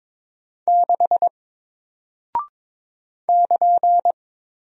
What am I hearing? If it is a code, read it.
Morse code, heard as a keyed tone, about 23 wpm, 700 Hz. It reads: 6 <KN>
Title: Single Character (Letter:Number:Punctuation:Prosign) Alt 22wpm